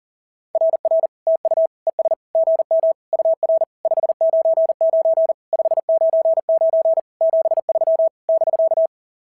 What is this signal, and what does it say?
Morse code, heard as a keyed tone, about 40 wpm, 650 Hz. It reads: RR TU ES GM UR 599 599 73 <BK>
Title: SOTA QSO Practice 40wpm